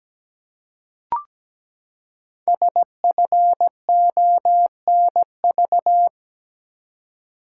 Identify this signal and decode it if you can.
Morse code, heard as a keyed tone, about 17 wpm, 700 Hz. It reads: SFONV
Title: Code Groups - 5 Letters - 17wpm